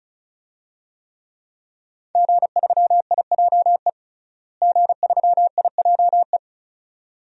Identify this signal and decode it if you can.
Morse code, heard as a keyed tone, about 36 wpm, 700 Hz. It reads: G3IJE G3IJE
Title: DXCC-100 Call Signs - 5 Characters - Repeated 2x - 35wpm